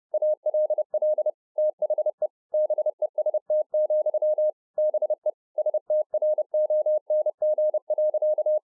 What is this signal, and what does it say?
Morse code, heard as a keyed tone, about 30 wpm, 610 Hz. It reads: ALL THE BEST, BE STRONG.